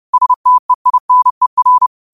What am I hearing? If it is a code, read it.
Morse code, heard as a keyed tone, about 30 wpm, 1000 Hz. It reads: STEINER
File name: qotdfiles / 30wpm